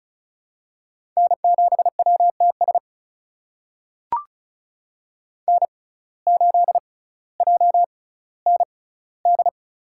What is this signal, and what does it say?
Morse code, heard as a keyed tone, about 35 wpm, spaced as for 18 wpm, 700 Hz. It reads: N7WTS N8JND
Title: USA Call Signs - 5 Characters - 6x Character Spacing - 35wpm